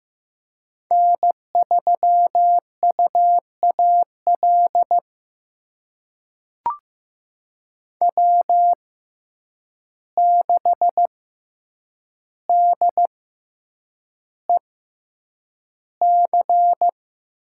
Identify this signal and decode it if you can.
Morse code, heard as a keyed tone, about 15 wpm, spaced as for 8 wpm, 700 Hz. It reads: N3UAL W6DEC
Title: USA Call Signs - 6x Character Spacing - 15wpm